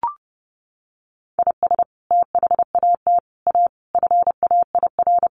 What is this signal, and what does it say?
Morse code, heard as a keyed tone, about 28 wpm, 700 Hz. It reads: IS THAT A FAIR
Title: Sentences from Top 1300 Words 30wpm